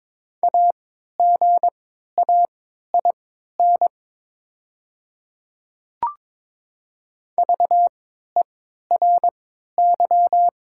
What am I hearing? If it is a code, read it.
Morse code, heard as a keyed tone, about 22 wpm, spaced as for 12 wpm, 700 Hz. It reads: AGAIN VERY
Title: 3x Character Spacing - Top 300 Words - 22wpm